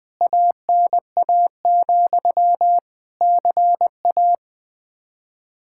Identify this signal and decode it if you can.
Morse code, heard as a keyed tone, about 20 wpm, 700 Hz. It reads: ANA, CA
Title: K1USN SST QSO Practice 20wpm